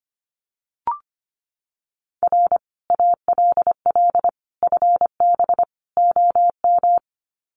Morse code, heard as a keyed tone, about 25 wpm, 700 Hz. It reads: R ALL FB OM
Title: Common QSO Phrases 25wpm